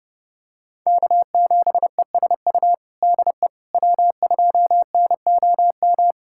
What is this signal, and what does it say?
Morse code, heard as a keyed tone, about 32 wpm, 700 Hz. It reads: K7ESU DE W2NOM
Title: USA Callsign DE Callsign - 5 Characters - 30wpm